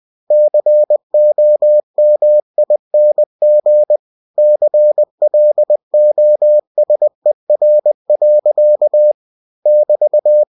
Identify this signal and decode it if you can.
Morse code, heard as a keyed tone, about 20 wpm, 600 Hz. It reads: COMING CLOSER. <BT>
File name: winnie-20wpm